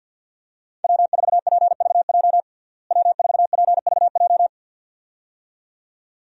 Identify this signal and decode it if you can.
Morse code, heard as a keyed tone, about 50 wpm, 700 Hz. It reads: W4PVJ W4PVJ
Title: USA Call Signs - 5 Characters - Repeated 2x - 50wpm